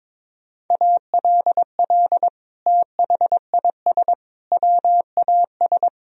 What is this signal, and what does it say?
Morse code, heard as a keyed tone, about 22 wpm, 700 Hz. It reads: ALL THIS WAS
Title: Sentences from Top 1000 Words - Review Part 4 - 22wpm